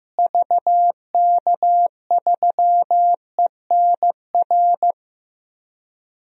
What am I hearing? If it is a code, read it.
Morse code, heard as a keyed tone, about 15 wpm, 700 Hz. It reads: VK3ENR
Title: DXCC-100 Call Signs - 6 Characters - Repeated 3x - 15wpm